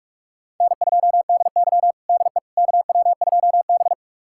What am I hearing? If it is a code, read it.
Morse code, heard as a keyed tone, about 46 wpm, 700 Hz. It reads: N2DY DE KW2B
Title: USA Callsign DE Callsign - 4 Characters - 45wpm